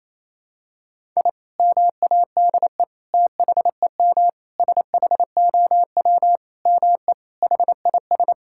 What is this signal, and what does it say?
Morse code, heard as a keyed tone, about 28 wpm, 700 Hz. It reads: I MADE THEM SHOW ME HIS
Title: Sentences from Top 1000 Words - Review Part 6 - 28wpm